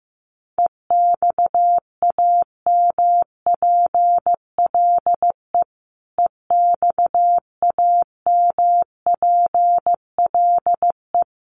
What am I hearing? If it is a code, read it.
Morse code, heard as a keyed tone, about 15 wpm, 700 Hz. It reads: EXAMPLE EXAMPLE
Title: Top 500 Words Repeated 15wpm